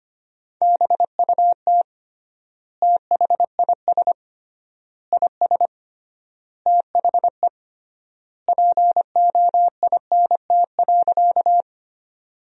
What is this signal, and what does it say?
Morse code, heard as a keyed tone, about 25 wpm, 700 Hz. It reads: BUT THIS IS THE POINT.
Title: 3x Word-Spacing - Sentences from Top 200 Words - 25wpm